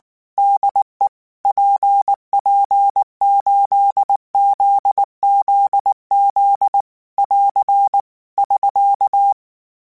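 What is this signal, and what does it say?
Morse code, heard as a keyed tone, about 19 wpm, 800 Hz. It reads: DE PP8ZZZ <AR> <SK>
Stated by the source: Morse code practice text at 18 wpm, from Portal do CW.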